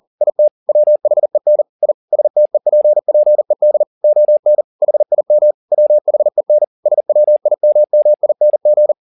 Morse code, heard as a keyed tone, about 40 wpm, 600 Hz. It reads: IT WHEN I STEPPED ON HIM WHEN SWIMMING
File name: the old man and the sea 2 40wpm